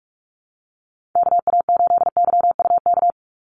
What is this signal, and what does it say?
Morse code, heard as a keyed tone, about 46 wpm, 700 Hz. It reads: KA8YUK
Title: USA Call Signs 45wpm